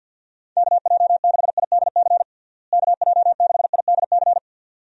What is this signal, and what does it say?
Morse code, heard as a keyed tone, about 46 wpm, 700 Hz. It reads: KJ6IDC KJ6IDC
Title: USA Call Signs - 6 Characters - Repeated 2x - 50wpm